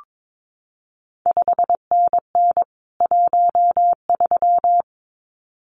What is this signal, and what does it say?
Morse code, heard as a keyed tone, about 22 wpm, 700 Hz. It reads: 5NN 13
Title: CQ WW DX Contest Exchange Encore - 22wpm